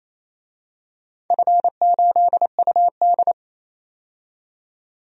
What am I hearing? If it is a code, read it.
Morse code, heard as a keyed tone, about 28 wpm, 700 Hz. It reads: F8UD
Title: DXCC-100 Call Signs - 4 Characters - 28wpm